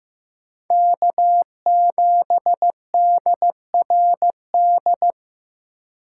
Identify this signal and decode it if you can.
Morse code, heard as a keyed tone, about 15 wpm, 700 Hz. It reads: K7DRD
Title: USA Call Signs - 5 Characters - 2x Character Spacing - 15wpm